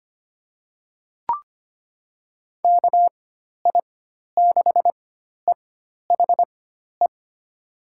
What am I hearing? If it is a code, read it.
Morse code, heard as a keyed tone, about 25 wpm, spaced as for 12 wpm, 700 Hz. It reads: KI6EHE
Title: USA Call Signs - 6 Characters - 4x Character Spacing - 25wpm